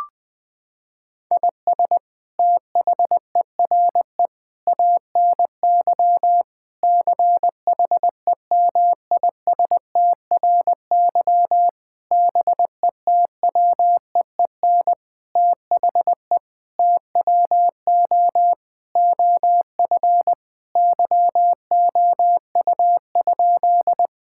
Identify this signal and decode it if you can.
Morse code, heard as a keyed tone, about 21 wpm, 700 Hz. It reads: IS THERE ANY CHEMISTRY BETWEEN THE TWO OF YOU?
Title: Sentences from Top 1000 Words - Review Part 6 - 20wpm